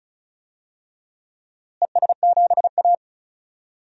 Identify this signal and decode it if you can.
Morse code, heard as a keyed tone, about 36 wpm, 700 Hz. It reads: ES7A
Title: DXCC-100 Call Signs - 4 Characters - Repeated 3x - 35wpm